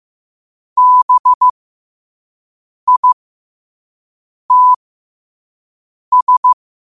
Morse code, heard as a keyed tone, about 15 wpm, spaced as for 5 wpm, 1000 Hz. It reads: BITS
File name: qotdfiles / 5wpm